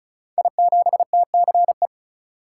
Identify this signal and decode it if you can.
Morse code, heard as a keyed tone, about 36 wpm, 700 Hz. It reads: I7TCE
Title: Code Groups - 5 Letter:Numbers - 35wpm